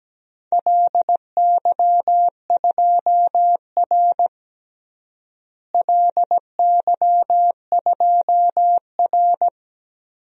Morse code, heard as a keyed tone, about 17 wpm, 700 Hz. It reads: LY2R LY2R
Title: DXCC-100 Call Signs - 4 Characters - Repeated 2x - 17wpm